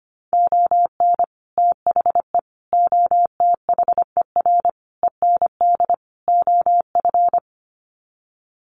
Morse code, heard as a keyed tone, about 25 wpm, 700 Hz. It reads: ON THE OTHER END OF
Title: Sets of 5 Words Encore 25wpm